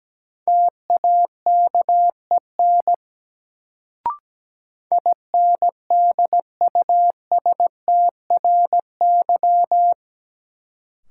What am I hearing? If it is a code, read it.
Morse code, heard as a keyed tone, about 17 wpm, 700 Hz. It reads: TAKEN INDUSTRY
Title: Sentences from Top 1000 Words - Review Part 4 - New Word Drill - 17wpm